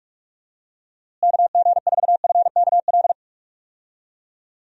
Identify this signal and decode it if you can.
Morse code, heard as a keyed tone, about 46 wpm, 700 Hz. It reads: KG4FKL
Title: USA Call Signs - 6 Characters - Repeated 2x - 45wpm